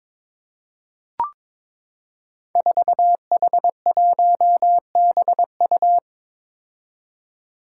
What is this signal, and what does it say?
Morse code, heard as a keyed tone, about 22 wpm, 700 Hz. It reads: 4H1BU
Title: Five Challenging Letter-Number Combinations 22wpm